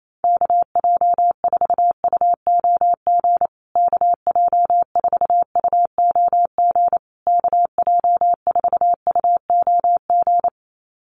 Morse code, heard as a keyed tone, about 28 wpm, 700 Hz. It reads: KJ4UOG KJ4UOG KJ4UOG
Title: USA Call Signs Repeated 3x 28wpm